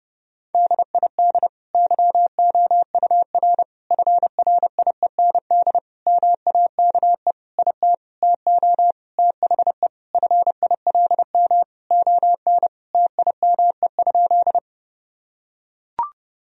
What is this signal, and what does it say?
Morse code, heard as a keyed tone, about 30 wpm, 700 Hz. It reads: DID YOUR FRIEND MAKE IT TO THE FILM ON TIME?
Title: Sentences from Top 1000 Words - Review Part 3 - 30wpm